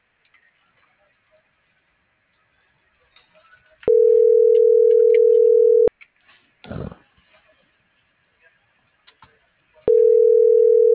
Ringback tone (North America), 440 + 480 Hz.